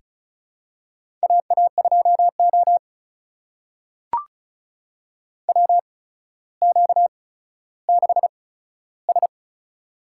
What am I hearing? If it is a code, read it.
Morse code, heard as a keyed tone, about 34 wpm, spaced as for 15 wpm, 700 Hz. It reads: AA2O WQ6S
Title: USA Call Signs - 4 Characters - 8x Character Spacing - 35wpm